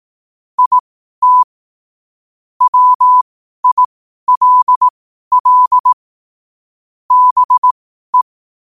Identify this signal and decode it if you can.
Morse code, heard as a keyed tone, about 18 wpm, spaced as for 12 wpm, 1000 Hz. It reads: IT WILL BE
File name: qotdfiles / 12wpm